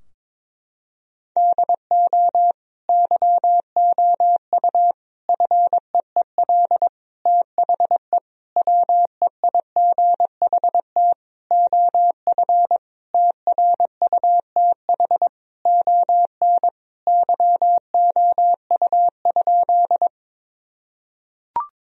Morse code, heard as a keyed tone, about 22 wpm, 700 Hz. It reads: DO YOU FEEL THE WEIGHT OF TRUTH ON YOU?
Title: Sentences from Top 1000 Words - Review Part 3 - 22wpm